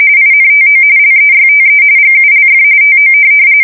RTTY, about 45.45 baud. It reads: INTO LEYTE GULF TO
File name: fsk170 / USS-New-Jersey